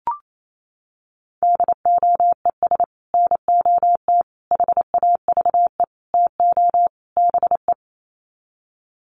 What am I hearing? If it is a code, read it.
Morse code, heard as a keyed tone, about 28 wpm, 700 Hz. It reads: DOES NOT HAVE TO BE
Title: Sets of 5 Words 28wpm